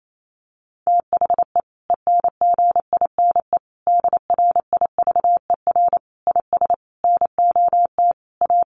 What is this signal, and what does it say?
Morse code, heard as a keyed tone, about 28 wpm, 700 Hz. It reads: THE ENGINE DRIVER IS NOT A
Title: Sentences from Top 1500 Words 28wpm